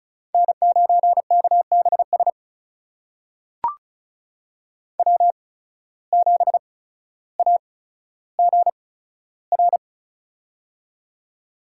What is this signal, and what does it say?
Morse code, heard as a keyed tone, about 36 wpm, spaced as for 15 wpm, 700 Hz. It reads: N9KBS W7AGR
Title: USA Call Signs - 5 Characters - 8x Character Spacing - 35wpm